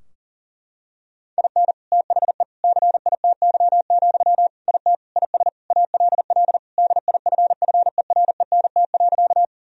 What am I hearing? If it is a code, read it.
Morse code, heard as a keyed tone, about 40 wpm, 700 Hz. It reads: IN THE CITY, IT IS ALL DIFFERENT.
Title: Sentences from Top 1000 Words - Review Part 3 - 40wpm